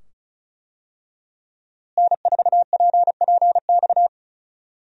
Morse code, heard as a keyed tone, about 36 wpm, 700 Hz. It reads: N4PPX
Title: USA Call Signs - 5 Characters - Repeated 2x - 35wpm